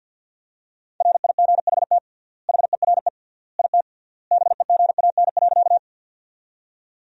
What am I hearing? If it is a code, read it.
Morse code, heard as a keyed tone, about 52 wpm, 700 Hz. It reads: RIGHT HERE IT BEGAN.
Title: 3x Word-Spacing - Sentences from Top 300 Words - 50wpm